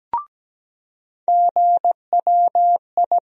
Morse code, heard as a keyed tone, about 17 wpm, 700 Hz. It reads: GWI
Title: USA Call Sign Postfixes - 3 Letters 17wpm